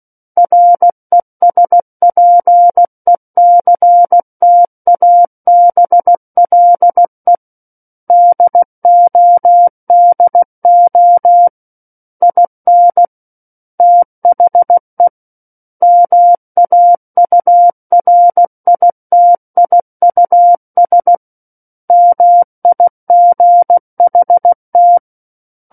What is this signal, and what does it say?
Morse code, heard as a keyed tone, about 16 wpm, 700 Hz. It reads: RESPECTABLE DODO IN THE MAURITIUS MIGHT
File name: Book One / 07-Chapter 7 - 16 WPM